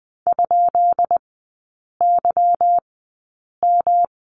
Morse code, heard as a keyed tone, about 20 wpm, spaced as for 8 wpm, 690 Hz. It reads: ?YM
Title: 145 Lesson 24e - Letter y - Three Characters 20wpm